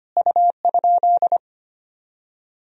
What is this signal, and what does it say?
Morse code, heard as a keyed tone, about 25 wpm, 700 Hz. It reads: U?
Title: Sentences from Top 200 Words and Abbreviations 25wpm